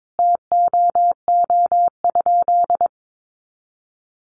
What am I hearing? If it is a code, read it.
Morse code, heard as a keyed tone, about 22 wpm, 700 Hz. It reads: TOO?
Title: Sentences from Top 1100 Words 22wpm